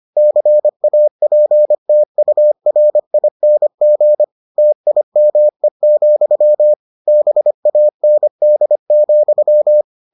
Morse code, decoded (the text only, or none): CAPTURING TIME, BAND,